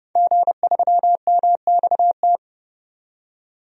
Morse code, heard as a keyed tone, about 30 wpm, 700 Hz. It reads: G3MXT
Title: DXCC-100 Call Signs - 5 Characters - 30wpm